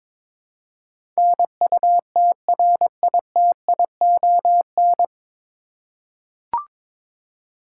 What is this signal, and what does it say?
Morse code, heard as a keyed tone, about 21 wpm, 700 Hz. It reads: NUTRITION
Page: Morse Code Ninja practice 22wpm